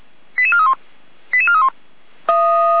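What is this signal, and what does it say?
Five-tone selective call (ZVEI), heard as quick runs of five tones, tones of about 70 ms. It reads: 88431 88431
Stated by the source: ZVEI five-tone call 88261, received by SDR.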